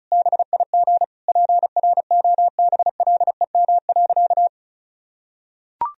Morse code, heard as a keyed tone, about 36 wpm, 700 Hz. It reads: BIG PROBLEM.
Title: Sentences from Top 1000 Words - Review Part 4 - 35wpm